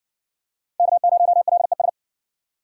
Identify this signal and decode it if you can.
Morse code, heard as a keyed tone, about 60 wpm, 700 Hz. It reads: K0LES